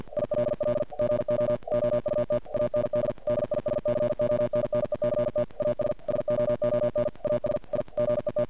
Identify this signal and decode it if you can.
Morse code, heard as a keyed tone, about 45 wpm, 590 Hz. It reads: ILL GO OUT AND DISCONNECT AS SOON AS I GET
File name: archived / CRSnet